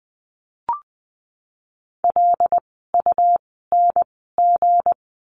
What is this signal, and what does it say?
Morse code, heard as a keyed tone, about 20 wpm, 700 Hz. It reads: L U N G
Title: 160 Lesson 26f - Letter g - Words 20wpm